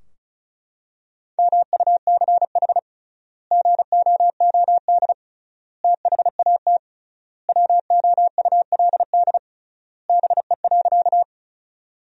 Morse code, heard as a keyed tone, about 34 wpm, 700 Hz. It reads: MUCH GOOD THAT WOULD BE.
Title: 3x Word-Spacing - Sentences from Top 200 Words - 35wpm